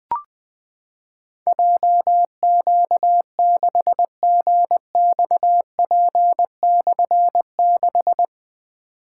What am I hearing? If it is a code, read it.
Morse code, heard as a keyed tone, about 20 wpm, 700 Hz. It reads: JQ6GXP/6
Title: Difficult Call Signs 20wpm